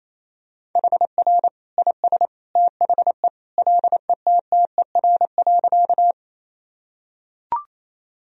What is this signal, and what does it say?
Morse code, heard as a keyed tone, about 28 wpm, 700 Hz. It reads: HR IS THE LETTER.
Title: Sentences from Top 300 Words and Abbreviations 28wpm